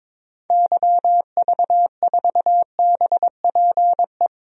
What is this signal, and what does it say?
Morse code, heard as a keyed tone, about 22 wpm, 700 Hz. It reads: YV4BPE
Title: DXCC-100 Call Signs - 6 Characters - 22wpm